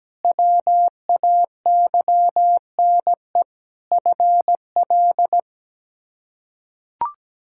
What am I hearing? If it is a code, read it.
Morse code, heard as a keyed tone, about 17 wpm, 700 Hz. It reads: WAYNE FL
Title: K1USN SST QSO Practice - 17wpm